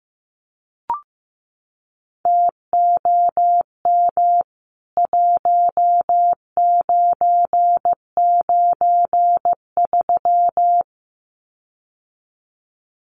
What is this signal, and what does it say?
Morse code, heard as a keyed tone, about 15 wpm, 700 Hz. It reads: TOM 1993
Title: CWT Exchange 15wpm